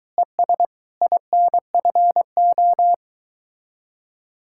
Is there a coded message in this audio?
Yes: Morse code from 0.2 to 2.9 s, heard as a keyed tone, about 23 wpm, 700 Hz. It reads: ES INFO